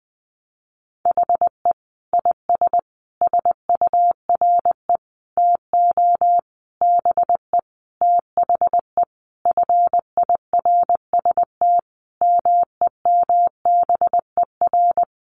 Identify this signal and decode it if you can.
Morse code, heard as a keyed tone, about 20 wpm, 700 Hz. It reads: HE IS SURE TO BE THE FIRST MEMBER
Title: Sentences from Top 1400 Words 20wpm